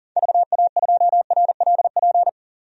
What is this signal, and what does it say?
Morse code, heard as a keyed tone, about 40 wpm, 700 Hz. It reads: VA2RLP
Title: DXCC-100 Call Signs - 6 Characters - 40wpm